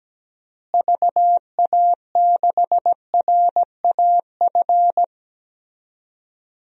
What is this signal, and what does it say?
Morse code, heard as a keyed tone, about 17 wpm, 700 Hz. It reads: VA6RAF
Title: DXCC-100 Call Signs - 6 Characters - Repeated 2x - 17wpm